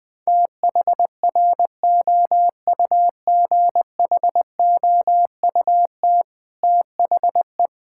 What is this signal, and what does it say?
Morse code, heard as a keyed tone, about 21 wpm, 700 Hz. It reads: THROUGHOUT THE
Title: Sentences from Top 1000 Words - Review Part 5 - 20wpm